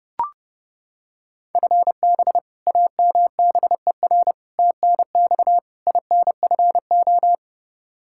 Morse code, heard as a keyed tone, about 32 wpm, 700 Hz. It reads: FB AMBER TNX INFO
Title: Practice QSO - Exchange 3 - More Info - 30wpm